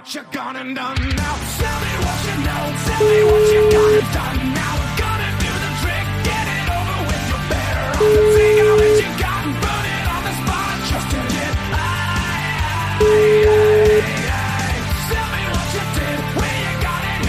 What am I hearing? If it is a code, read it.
Ringback tone (Europe, 425 Hz), with noise.